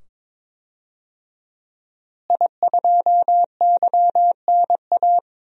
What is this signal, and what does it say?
Morse code, heard as a keyed tone, about 21 wpm, 700 Hz. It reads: I2YNA
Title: DXCC-100 Call Signs - 5 Characters - Repeated 2x - 22wpm